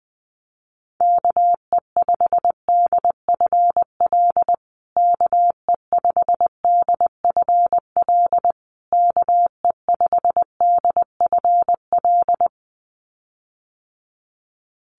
Morse code, heard as a keyed tone, about 20 wpm, 700 Hz. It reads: KE5DFL KE5DFL KE5DFL
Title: USA Call Signs Repeated 3x 20wpm